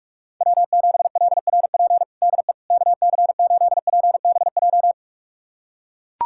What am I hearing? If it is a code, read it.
Morse code, heard as a keyed tone, about 41 wpm, 700 Hz. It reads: W7LRP DE KC8PBJ
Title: USA Callsign DE Callsign 45wpm